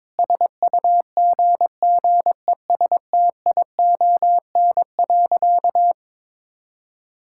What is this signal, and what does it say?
Morse code, heard as a keyed tone, about 22 wpm, 700 Hz. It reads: SUGGESTION.
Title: Sentences from Top 1600 Words 22wpm